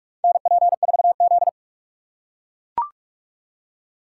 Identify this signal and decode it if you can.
Morse code, heard as a keyed tone, about 41 wpm, 700 Hz. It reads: NP4Z
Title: USA Call Signs - 4 Characters - 2x Character Spacing - 45wpm